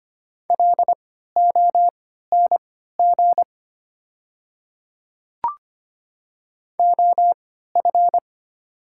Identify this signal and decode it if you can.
Morse code, heard as a keyed tone, about 25 wpm, spaced as for 14 wpm, 700 Hz. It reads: LONG OF
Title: 3x Character Spacing - Top 100 Words - 25wpm